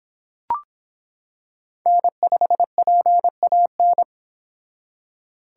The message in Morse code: N5PAN

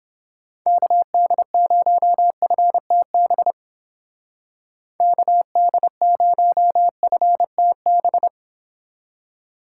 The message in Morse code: KD0FTB KD0FTB

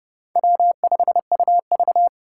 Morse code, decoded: W5UV

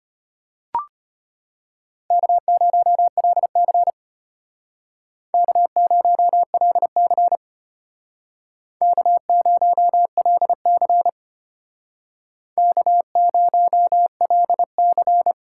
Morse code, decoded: K0LC K0LC K0LC K0LC

